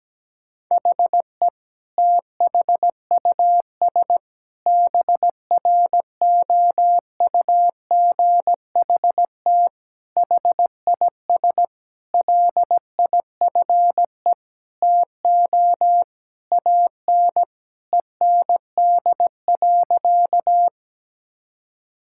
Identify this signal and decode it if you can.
Morse code, heard as a keyed tone, about 17 wpm, 700 Hz. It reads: HE THUS BROUGHT HIS LIFE TO AN END.
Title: Sentences from Top 1000 Words - Review Part 5 - 17wpm